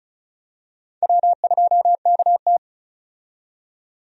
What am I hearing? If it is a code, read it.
Morse code, heard as a keyed tone, about 34 wpm, 700 Hz. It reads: W2KT
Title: USA Call Signs - 4 Characters - 35wpm